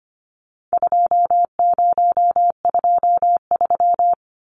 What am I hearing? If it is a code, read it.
Morse code, heard as a keyed tone, about 25 wpm, 700 Hz. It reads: 2023
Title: Numbers - 4-Digit 25wpm